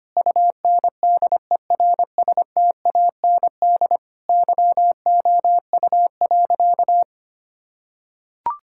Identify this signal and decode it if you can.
Morse code, heard as a keyed tone, about 25 wpm, 700 Hz. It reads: UNDERSTAND YOU.
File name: practice / 25wpm / Sentences with 4-Word Phrases - 25wpm